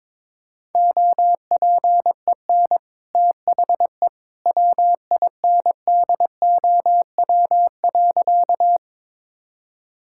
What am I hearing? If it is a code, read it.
Morse code, heard as a keyed tone, about 22 wpm, 690 Hz. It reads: OPEN THE WINDOW.
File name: practice / 22wpm / Easy Sentences with Hints - 22wpm